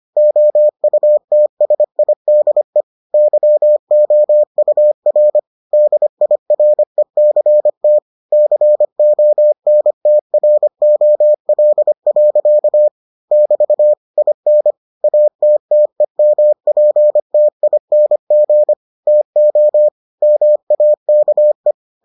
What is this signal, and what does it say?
Morse code, heard as a keyed tone, about 25 wpm, 600 Hz. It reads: OUTSIDE YOUR DIRECT CONTROL. <BT> IN ATTEMPTING TO MAKE